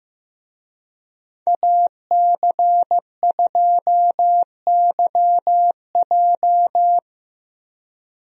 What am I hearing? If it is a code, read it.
Morse code, heard as a keyed tone, about 15 wpm, 700 Hz. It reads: AC2YJ
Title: USA Call Signs - 5 Characters - Repeated 2x - 15wpm